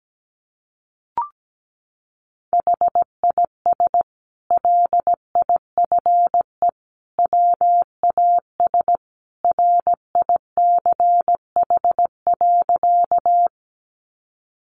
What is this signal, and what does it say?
Morse code, heard as a keyed tone, about 17 wpm, 700 Hz. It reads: HIS LIFE WAS RICH.
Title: Sentences from Top 1100 Words 17wpm